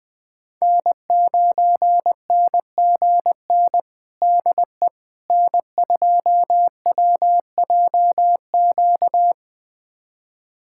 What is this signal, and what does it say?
Morse code, heard as a keyed tone, about 20 wpm, 700 Hz. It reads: N9NGN DE N2WJQ
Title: USA Callsign DE Callsign 20wpm